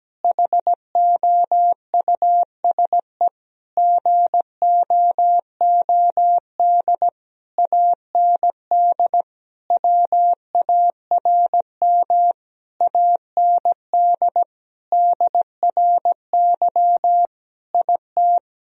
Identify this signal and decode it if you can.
Morse code, heard as a keyed tone, about 17 wpm, 700 Hz. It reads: HOUSE GOOD AND WARM AND DRY IT
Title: Sentences from Top 1000 Words - Review Part 3 - 17wpm